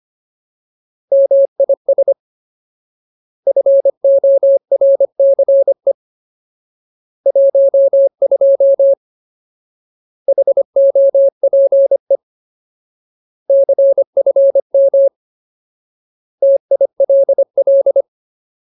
Morse code, heard as a keyed tone, about 25 wpm, 550 Hz. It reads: MIS FORCE 12 HOPE CFM TILL
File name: Ultimate Word List / 25WPM Ultimate